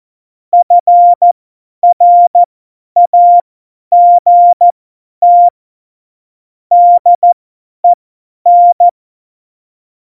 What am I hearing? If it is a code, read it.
Morse code, heard as a keyed tone, about 14 wpm, spaced as for 10 wpm, 700 Hz. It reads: FRAGT DEN
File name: Morsekurs / Witze / Witz 11 August 2020, 50 F70